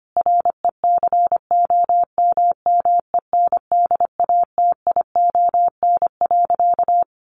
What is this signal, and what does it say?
Morse code, heard as a keyed tone, about 25 wpm, 700 Hz. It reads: RECOMMENDATION.
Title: Sentences from Top 1400 Words 25wpm